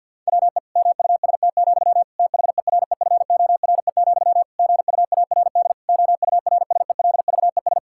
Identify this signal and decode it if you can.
Morse code, heard as a keyed tone, about 49 wpm, 700 Hz. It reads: WE MUST, THEREFORE, GUARD OURSELVES
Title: Sentences from Top 1600 Words 50wpm